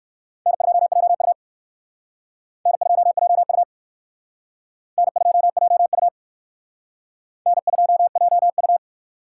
Morse code, heard as a keyed tone, about 56 wpm, 700 Hz. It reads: N2JU N2JU N2JU N2JU
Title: Speed Racing - USA 1x2 Call Signs 45wpm